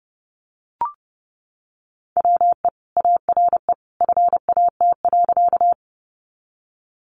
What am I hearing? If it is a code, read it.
Morse code, heard as a keyed tone, about 30 wpm, 700 Hz. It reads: WE ARE FAT.
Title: Sentences from Top 600 Words 30wpm